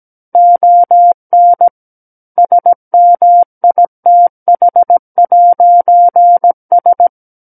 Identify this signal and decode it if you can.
Morse code, heard as a keyed tone, about 17 wpm, 700 Hz. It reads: ON SMITH'S
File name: Book One / 08-Chapter 8 - 17 WPM